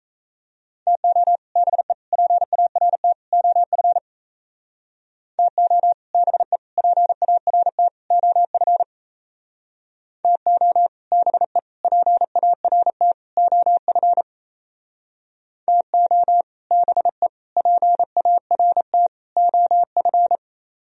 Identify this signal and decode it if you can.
Morse code, heard as a keyed tone, about 37 wpm, 700 Hz. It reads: TO BE PART OF TO BE PART OF TO BE PART OF TO BE PART OF